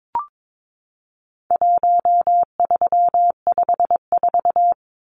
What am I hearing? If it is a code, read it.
Morse code, heard as a keyed tone, about 22 wpm, 700 Hz. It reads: 1354
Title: Numbers - 4-Digit 22wpm